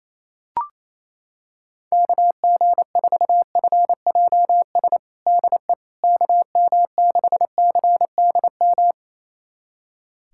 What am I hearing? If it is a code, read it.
Morse code, heard as a keyed tone, about 28 wpm, 700 Hz. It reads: KG4FJS DE KM6CDM
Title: USA Callsign DE Callsign 28wpm